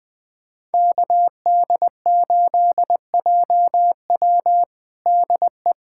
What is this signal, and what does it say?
Morse code, heard as a keyed tone, about 21 wpm, 700 Hz. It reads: KD8JW DE
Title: USA Callsign DE Callsign - 5 Characters - 20wpm